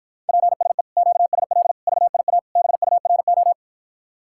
Morse code, heard as a keyed tone, about 53 wpm, 700 Hz. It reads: PSE QSL VIA BURO